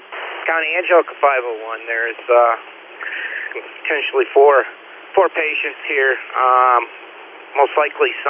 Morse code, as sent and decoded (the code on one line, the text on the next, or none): none